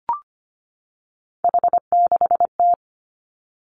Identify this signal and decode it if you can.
Morse code, heard as a keyed tone, about 25 wpm, 700 Hz. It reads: H6T
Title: Three Challenging Letter-Number Combinations 25wpm